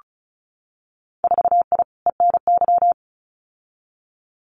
Morse code, heard as a keyed tone, about 34 wpm, 700 Hz. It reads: 4I ENY
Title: Winter Field Day Exchanges 35wpm